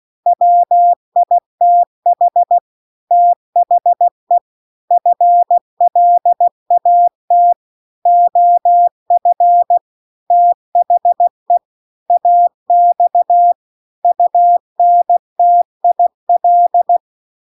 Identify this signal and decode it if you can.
Morse code, heard as a keyed tone, about 16 wpm, 700 Hz. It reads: WITH THE FLAT OF THE AX UNTIL